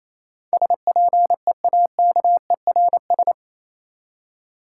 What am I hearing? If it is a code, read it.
Morse code, heard as a keyed tone, about 27 wpm, 700 Hz. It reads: SPEAKERS
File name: Easy Words - Speed-Racing with Hints - 22wpm